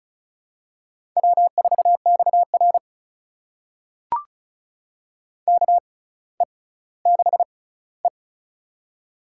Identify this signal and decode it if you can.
Morse code, heard as a keyed tone, about 36 wpm, spaced as for 18 wpm, 700 Hz. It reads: W4XR KE6E
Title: USA Call Signs - 4 Characters - 6x Character Spacing - 35wpm